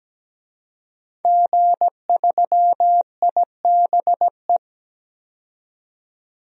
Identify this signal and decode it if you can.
Morse code, heard as a keyed tone, about 17 wpm, 700 Hz. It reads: G3IBE
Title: DXCC-100 Call Signs - 5 Characters - 17wpm